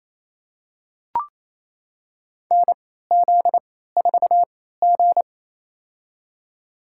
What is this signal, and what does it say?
Morse code, heard as a keyed tone, about 28 wpm, spaced as for 16 wpm, 700 Hz. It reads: NZ4G
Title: USA Call Signs - 4 Characters - 3x Character Spacing - 28wpm